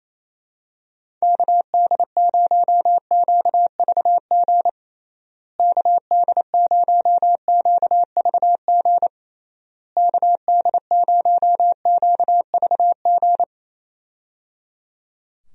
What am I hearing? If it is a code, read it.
Morse code, heard as a keyed tone, about 28 wpm, 700 Hz. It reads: KD0QVG KD0QVG KD0QVG
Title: USA Call Signs - 6 Characters - Repeated 3x - 28wpm